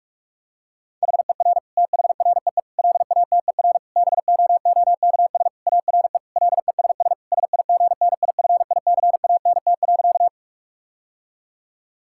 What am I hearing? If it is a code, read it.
Morse code, heard as a keyed tone, about 46 wpm, 700 Hz. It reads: HER THREE LATER BOOKS ARE LESS SIGNIFICANT.
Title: Sentences from Top 1000 Words - Review Part 6 - 45wpm